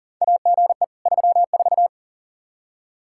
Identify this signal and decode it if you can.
Morse code, heard as a keyed tone, about 40 wpm, 700 Hz. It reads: AGE 34
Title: K1USN SST QSO Practice 40wpm